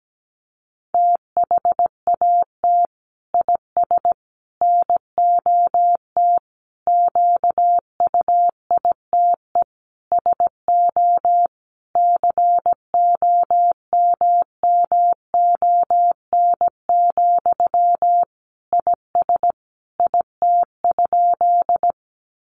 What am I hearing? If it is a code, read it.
Morse code, heard as a keyed tone, about 17 wpm, 700 Hz. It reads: THAT IS NOT QUITE SO COMMON, IS IT?
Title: Sentences from Top 700 Words 17wpm